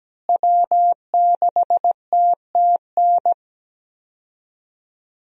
Morse code, heard as a keyed tone, about 17 wpm, 700 Hz. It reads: W6TTN
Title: USA Call Signs - 5 Characters - Repeated 2x - 17wpm